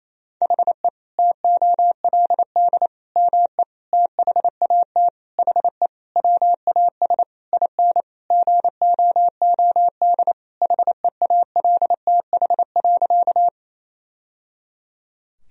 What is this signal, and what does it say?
Morse code, heard as a keyed tone, about 28 wpm, 700 Hz. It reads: HE TOLD ME THAT HE WAS IN GOOD HEALTH.
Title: Sentences from Top 1000 Words - Review Part 3 - 28wpm